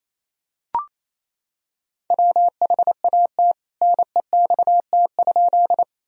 Morse code, heard as a keyed tone, about 28 wpm, 700 Hz. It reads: WHAT NEXT?
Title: Sentences from Top 1000 Words - Review Part 1 - 28wpm